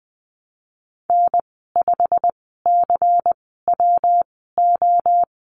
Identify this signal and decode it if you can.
Morse code, heard as a keyed tone, about 19 wpm, spaced as for 14 wpm, 700 Hz. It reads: N5CWO
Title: 122 Lesson 20g - Number 3 - Call Signs 20wpm